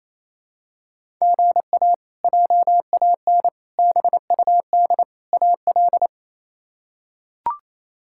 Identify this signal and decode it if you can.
Morse code, heard as a keyed tone, about 27 wpm, 700 Hz. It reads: GA JAN BUD AL
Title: K1USN SST QSO Practice - 28wpm